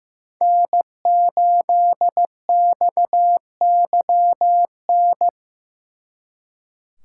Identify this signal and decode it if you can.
Morse code, heard as a keyed tone, about 18 wpm, 700 Hz. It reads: N8XYN
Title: Speed Racing - USA Call Signs 15wpm